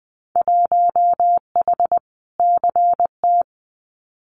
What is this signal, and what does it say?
Morse code, heard as a keyed tone, about 20 wpm, 700 Hz. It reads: 1H CT
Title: Winter Field Day Exchanges 20wpm